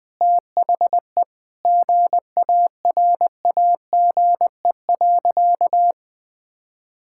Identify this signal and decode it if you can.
Morse code, heard as a keyed tone, about 20 wpm, 700 Hz. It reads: THE GARAGE.
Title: Sentences from Top 1600 Words 20wpm